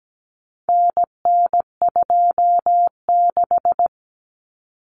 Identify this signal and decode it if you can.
Morse code, heard as a keyed tone, about 17 wpm, 700 Hz. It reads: NN26
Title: Common Cut Numbers - 4-Digit 17wpm